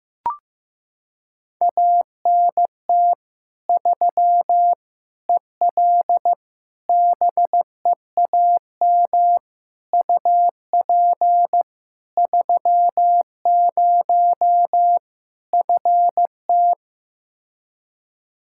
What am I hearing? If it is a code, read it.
Morse code, heard as a keyed tone, about 15 wpm, 700 Hz. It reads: ANT 3 EL BEAM UP 30 FT
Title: Practice QSO - Exchange 2 - Info - 15wpm